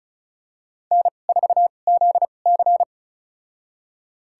Morse code, heard as a keyed tone, about 34 wpm, spaced as for 25 wpm, 700 Hz. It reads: N4ZC
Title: USA Call Signs - 2x Character Spacing - 35wpm